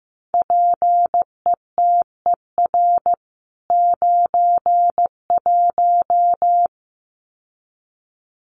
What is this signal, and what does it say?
Morse code, heard as a keyed tone, about 15 wpm, 700 Hz. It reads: PETER 91
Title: CWT Exchange 15wpm